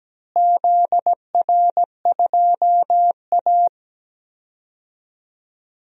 Morse code, heard as a keyed tone, about 17 wpm, 700 Hz. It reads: ZR2A
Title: DXCC-100 Call Signs - 4 Characters - Repeated 3x - 17wpm